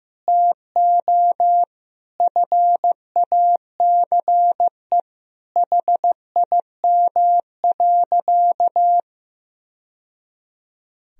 Morse code, heard as a keyed tone, about 16 wpm, 700 Hz. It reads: TO FACE HIM.
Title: Sentences from Top 1000 Words - Review Part 2 - 15wpm